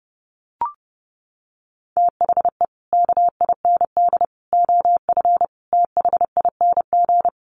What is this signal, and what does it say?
Morse code, heard as a keyed tone, about 30 wpm, 700 Hz. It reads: THE KIND OF THING
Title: Sets of 5 Words Encore 30wpm